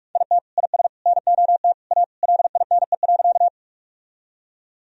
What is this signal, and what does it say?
Morse code, heard as a keyed tone, about 43 wpm, 700 Hz. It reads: IT IS NOT A LINE.
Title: Sentences from Top 1000 Words - Review Part 1 - 45wpm